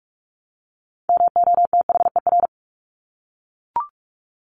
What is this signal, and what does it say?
Morse code, heard as a keyed tone, about 44 wpm, 700 Hz. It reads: MOTHER
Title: Top 200 Words 45wpm